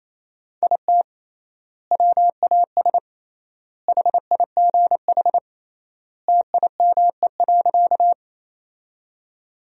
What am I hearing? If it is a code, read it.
Morse code, heard as a keyed tone, about 28 wpm, 700 Hz. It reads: IT WAS HIGH TIME.
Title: 3x Word-Spacing - Sentences from Top 200 Words - 28wpm